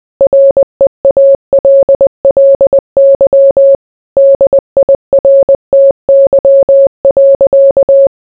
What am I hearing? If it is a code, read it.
Morse code, heard as a keyed tone, about 20 wpm, 550 Hz. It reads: REALLY DIRTY.